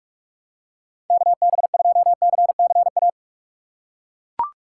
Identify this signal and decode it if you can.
Morse code, heard as a keyed tone, about 41 wpm, 700 Hz. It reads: KB2CCA
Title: USA Call Signs - 6 Characters - 8x Character Spacing - 45wpm